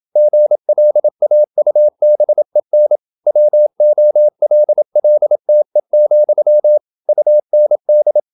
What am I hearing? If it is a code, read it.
Morse code, heard as a keyed tone, about 27 wpm, 600 Hz. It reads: GLAUBEN WOLLTE, UND